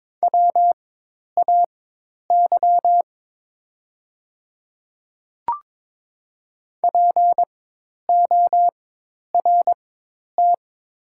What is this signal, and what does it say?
Morse code, heard as a keyed tone, about 22 wpm, spaced as for 10 wpm, 700 Hz. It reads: WAY PORT
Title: 4x Character Spacing - Top 300 Words - 22wpm